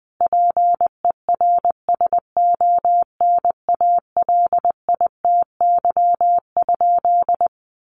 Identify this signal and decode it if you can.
Morse code, heard as a keyed tone, about 20 wpm, 700 Hz. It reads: PERSONALITY?
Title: Sentences from Top 1400 Words 20wpm